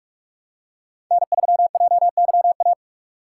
Morse code, heard as a keyed tone, about 46 wpm, 700 Hz. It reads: N3JYA
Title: USA Call Signs - 5 Characters - 3x Character Spacing - 45wpm